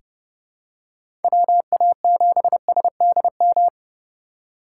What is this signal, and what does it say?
Morse code, heard as a keyed tone, about 30 wpm, 700 Hz. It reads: WA7SDM
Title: USA Call Signs - 6 Characters - 2x Character Spacing - 30wpm